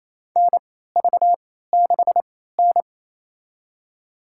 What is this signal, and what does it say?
Morse code, heard as a keyed tone, about 27 wpm, spaced as for 16 wpm, 700 Hz. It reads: NV6N
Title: USA Call Signs - 3x Character Spacing - 28wpm